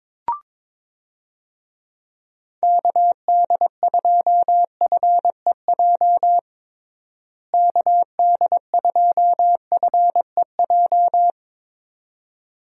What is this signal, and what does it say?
Morse code, heard as a keyed tone, about 22 wpm, 700 Hz. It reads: KD2FEJ KD2FEJ
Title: USA Call Signs - 6 Characters - Repeated 2x - 22wpm